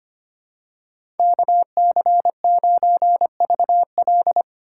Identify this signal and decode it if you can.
Morse code, heard as a keyed tone, about 25 wpm, 700 Hz. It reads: KC9VL
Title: USA Call Signs - 5 Characters - 3x Character Spacing - 25wpm